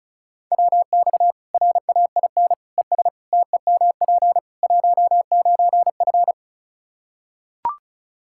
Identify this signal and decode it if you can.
Morse code, heard as a keyed tone, about 34 wpm, 700 Hz. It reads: WX RAIN ES TEMP 19F
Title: Practice QSO - Exchange 2 - Info - 35wpm